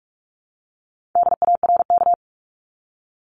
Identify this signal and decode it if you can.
Morse code, heard as a keyed tone, about 44 wpm, 700 Hz. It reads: DARK
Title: Top 500 Words Repeated 45wpm